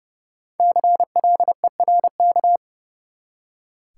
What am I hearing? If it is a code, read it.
Morse code, heard as a keyed tone, about 30 wpm, 700 Hz. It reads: CLERK
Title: Five Letter Words - 30wpm